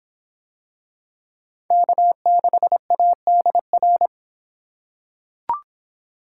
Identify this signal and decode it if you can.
Morse code, heard as a keyed tone, about 22 wpm, 700 Hz. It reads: K6ADR